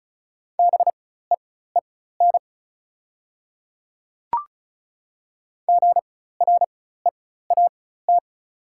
Morse code, heard as a keyed tone, about 36 wpm, spaced as for 16 wpm, 700 Hz. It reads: BEEN GREAT
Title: 4x Character Spacing - Top 200 Words - 35wpm